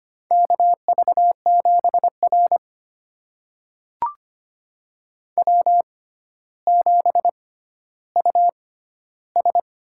Morse code, heard as a keyed tone, about 25 wpm, spaced as for 13 wpm, 700 Hz. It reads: KV7R W7US
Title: USA Call Signs - 4 Characters - 6x Character Spacing - 25wpm